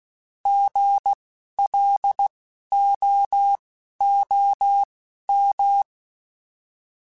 Morse code, heard as a keyed tone, about 16 wpm, spaced as for 11 wpm, 800 Hz. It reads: GLOOM
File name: Alphabetical / Alphabetical mp3s / Gs @10